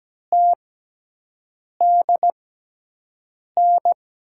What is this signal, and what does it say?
Morse code, heard as a keyed tone, about 17 wpm, spaced as for 6 wpm, 700 Hz. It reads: TDN
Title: USA Call Signs - 5 Characters - 6x Character Spacing - 17wpm